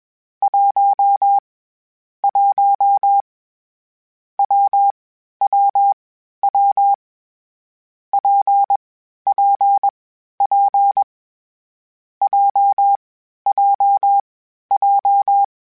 Morse code, heard as a keyed tone, about 21 wpm, 800 Hz. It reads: 1 1 W W W P P P J J J
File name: Difficult Combos / Difficult Combos-mp3s / WPJ WPJ1 @20